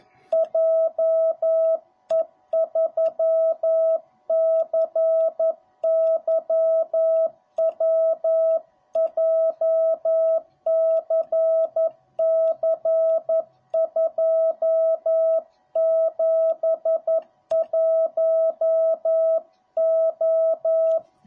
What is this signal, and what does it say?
Morse code, heard as a keyed tone, about 11 wpm, 660 Hz. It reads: JE3CYW JCC271O